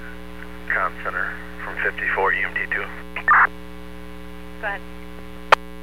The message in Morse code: none

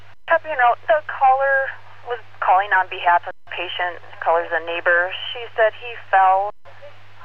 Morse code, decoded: none

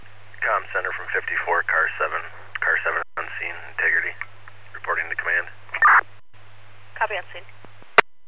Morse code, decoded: none

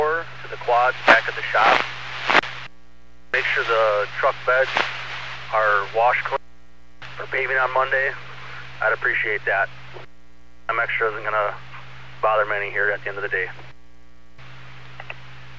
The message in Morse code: none

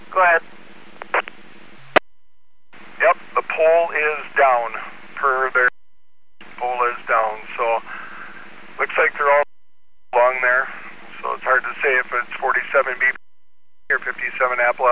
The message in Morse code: none

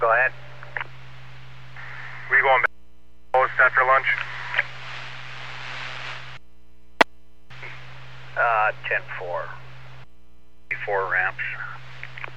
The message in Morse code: none

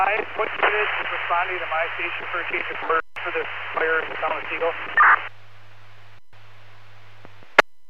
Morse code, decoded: none

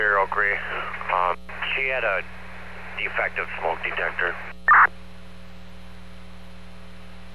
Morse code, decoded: none

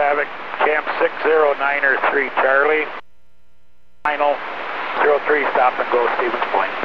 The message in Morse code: none

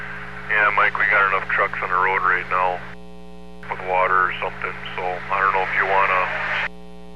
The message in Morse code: none

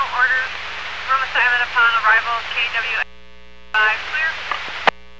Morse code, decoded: none

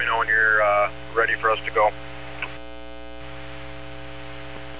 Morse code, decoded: none

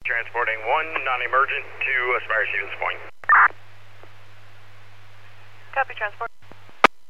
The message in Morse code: none

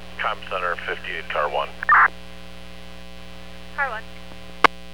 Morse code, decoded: none

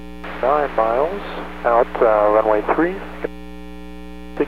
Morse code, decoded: none

